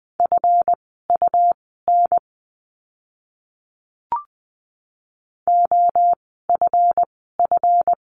Morse code, heard as a keyed tone, about 20 wpm, spaced as for 14 wpm, 700 Hz. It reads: FUN OFF
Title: 139 Lesson 23f - Letter f - Words 20wpm